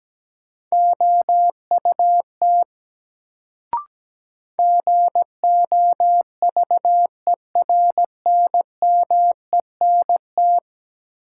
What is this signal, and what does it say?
Morse code, heard as a keyed tone, about 17 wpm, 700 Hz. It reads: OUT GOVERNMENT
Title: Sets of 2 Words - New Word Drill - 17wpm